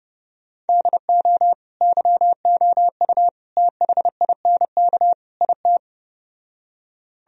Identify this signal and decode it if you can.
Morse code, heard as a keyed tone, about 31 wpm, 700 Hz. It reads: DO YOU THINK IT